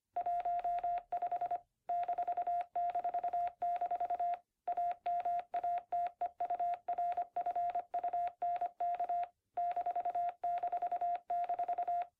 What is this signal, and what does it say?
Morse code, heard as a keyed tone, about 25 wpm, 720 Hz. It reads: 15 --- AMATEURFUNK ---